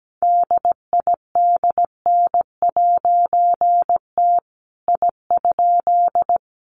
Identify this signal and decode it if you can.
Morse code, heard as a keyed tone, about 17 wpm, 700 Hz. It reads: DIDN'T I?
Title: A Sound Of Thunder 17wpm